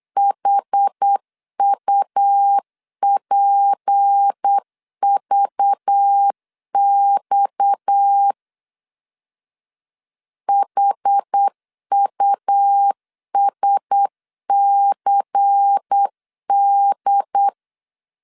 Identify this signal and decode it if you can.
Morse code, heard as a keyed tone, about 8 wpm, 800 Hz. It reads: HUPVX HUSCD
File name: LEZIONI / 7 GF9Z / 7 GF9Z 8wpm 3213